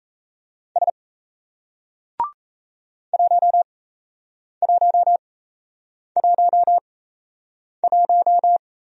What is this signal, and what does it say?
Morse code, heard as a keyed tone, about 38 wpm, spaced as for 8 wpm, 700 Hz. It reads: S 1111